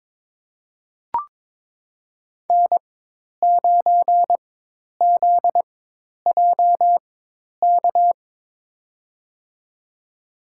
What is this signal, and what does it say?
Morse code, heard as a keyed tone, about 22 wpm, spaced as for 10 wpm, 700 Hz. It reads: N9ZJK
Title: USA Call Signs - 5 Characters - 4x Character Spacing - 22wpm